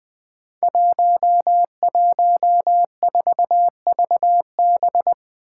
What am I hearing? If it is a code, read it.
Morse code, heard as a keyed tone, about 20 wpm, 700 Hz. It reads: 114VB
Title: Five Challenging Letter-Number Combinations 20wpm